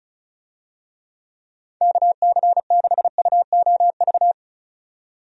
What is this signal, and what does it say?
Morse code, heard as a keyed tone, about 36 wpm, 700 Hz. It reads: KC6UOV
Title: USA Call Signs - 6 Characters - 35wpm